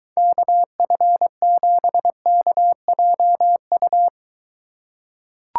Morse code, heard as a keyed tone, about 23 wpm, 700 Hz. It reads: KF7KJU